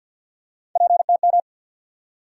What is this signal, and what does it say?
Morse code, heard as a keyed tone, about 47 wpm, 700 Hz. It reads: PTM
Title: USA Call Sign Postfixes - 3 Letters 50wpm